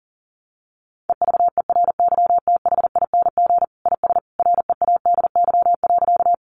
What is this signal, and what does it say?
Morse code, heard as a keyed tone, about 40 wpm, 700 Hz. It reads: EVERYTHING IS READY.
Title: Sentences from Top 700 Words 40wpm